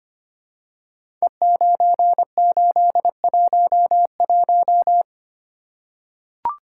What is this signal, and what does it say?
Morse code, heard as a keyed tone, about 25 wpm, 700 Hz. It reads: E9811
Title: Code Groups - 5 Letter:Numbers - 25wpm